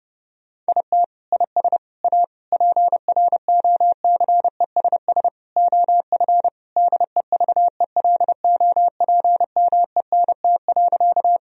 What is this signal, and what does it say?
Morse code, heard as a keyed tone, about 32 wpm, 700 Hz. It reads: IT IS A PROCESS OF DEVELOPMENT.
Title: Sentences from Top 1000 Words - Review Part 4 - 30wpm